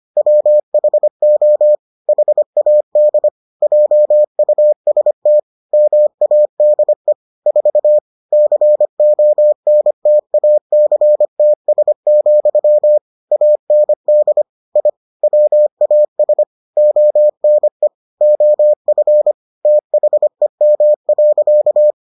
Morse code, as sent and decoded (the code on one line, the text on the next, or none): .-- .... --- / .... .- -.. / .--- ..- ... - / -- .- -.. / ....- / -.-. --- -. - .- -.-. - ... --..-- / .- -. -.. / .. / .-- .- ... / --- -. / --- ..-. / - .... -- .-.-.-
WHO HAD JUST MADE 4 CONTACTS, AND I WAS ONE OF THEM.